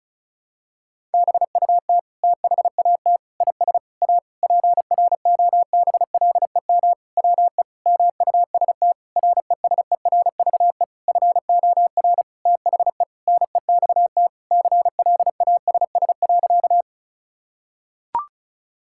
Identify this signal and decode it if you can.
Morse code, heard as a keyed tone, about 36 wpm, 700 Hz. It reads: BUT THAT IS A PROBLEM WE MUST RESERVE FOR THE NEXT CLASS.
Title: Sentences from Top 1000 Words - Review Part 6 - 35wpm